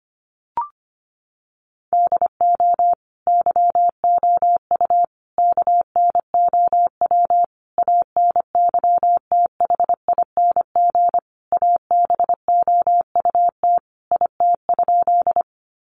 Morse code, heard as a keyed tone, about 25 wpm, 700 Hz. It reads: DO YOU KNOW ANYTHING ABOUT IT?
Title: Sentences from Top 1000 Words 25wpm